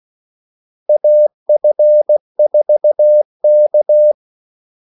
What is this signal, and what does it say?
Morse code, heard as a keyed tone, about 16 wpm, 600 Hz. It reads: AF4K
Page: Kaldesignaler 16 wpm